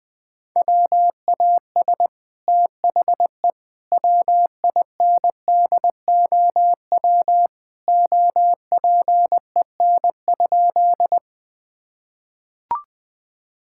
Morse code, heard as a keyed tone, about 20 wpm, 700 Hz. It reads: WAS THE WINDOW OPEN?
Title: Sentences from Top 1600 Words 20wpm